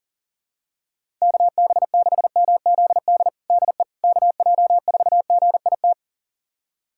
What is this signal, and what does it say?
Morse code, heard as a keyed tone, about 40 wpm, 700 Hz. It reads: KB6MZD DE KJ4GIT
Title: USA Callsign DE Callsign - 6 Characters - 40wpm